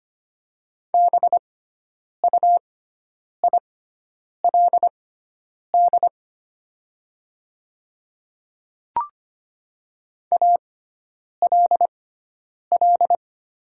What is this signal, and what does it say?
Morse code, heard as a keyed tone, about 25 wpm, spaced as for 9 wpm, 700 Hz. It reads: BUILD ALL
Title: 6x Character Spacing - Top 200 Words - 25wpm